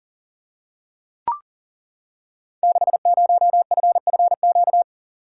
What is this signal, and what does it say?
Morse code, heard as a keyed tone, about 39 wpm, 690 Hz. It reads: 60FFQ
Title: Code Groups - 5 Letter:Numbers - 40wpm